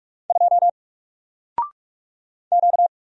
Morse code, heard as a keyed tone, about 47 wpm, 700 Hz. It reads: 2 Q
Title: Single Letter-Number Alt 45wpm